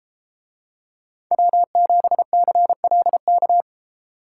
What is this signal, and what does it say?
Morse code, heard as a keyed tone, about 33 wpm, 700 Hz. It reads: W7CLK